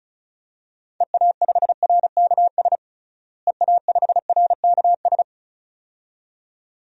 Morse code, heard as a keyed tone, about 36 wpm, 700 Hz. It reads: EA5RKS EA5RKS
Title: DXCC-100 Call Signs - 6 Characters - Repeated 2x - 35wpm